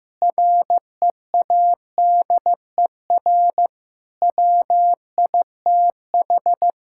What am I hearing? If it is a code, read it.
Morse code, heard as a keyed tone, about 15 wpm, 700 Hz. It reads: READER WITH
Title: Pangram Challenge v2 - 15wpm